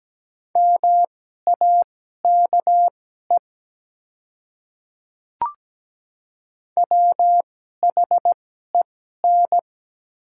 Morse code, heard as a keyed tone, about 17 wpm, spaced as for 12 wpm, 700 Hz. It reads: MAKE WHEN
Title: 2x Character Spacing - Top 100 Words - 17wpm